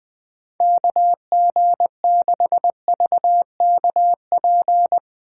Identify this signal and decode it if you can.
Morse code, heard as a keyed tone, about 20 wpm, 700 Hz. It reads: KG6VKP
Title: USA Call Signs - 6 Characters - 4x Character Spacing - 20wpm